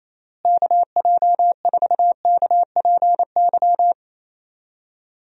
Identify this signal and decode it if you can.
Morse code, heard as a keyed tone, about 28 wpm, 700 Hz. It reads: KJ4KPY
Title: USA Call Signs - 6 Characters - Repeated 2x - 28wpm